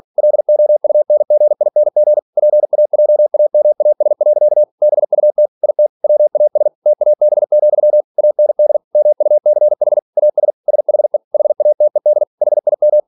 Morse code, heard as a keyed tone, about 46 wpm, 600 Hz. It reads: LOUNGING PAJAMAS. BUT IT WAS TAB, AND MUCH AS SHE HATED HIM
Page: Never 47wpm